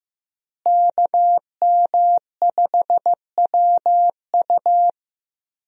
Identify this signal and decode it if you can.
Morse code, heard as a keyed tone, about 15 wpm, 700 Hz. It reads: KM5WU
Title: USA Call Signs - 5 Characters - Repeated 2x - 15wpm